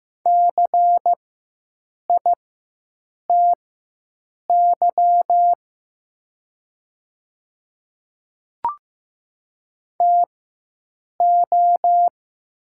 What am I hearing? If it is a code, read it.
Morse code, heard as a keyed tone, about 15 wpm, spaced as for 7 wpm, 700 Hz. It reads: CITY TO
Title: 4x Character Spacing - Top 300 Words - 15wpm